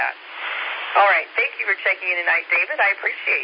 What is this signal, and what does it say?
Emergency Alert System (SAME) tones, heard as harsh buzzing bursts, with speech. It is heard from 0.3 to 3.2 s.
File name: SimplexLogic